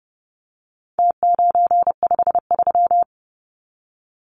Morse code, heard as a keyed tone, about 30 wpm, 700 Hz. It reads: T953
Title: Numbers - Contest Style - 4-Digit 30wpm